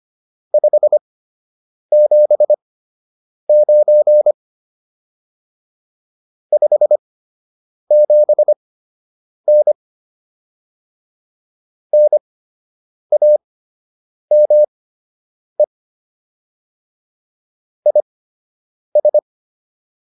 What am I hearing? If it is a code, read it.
Morse code, heard as a keyed tone, about 25 wpm, spaced as for 8 wpm, 600 Hz. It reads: 579 57N NAME IS